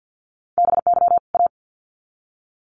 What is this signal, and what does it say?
Morse code, heard as a keyed tone, about 50 wpm, 700 Hz. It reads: BY A
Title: Sets of 2 Words 50wpm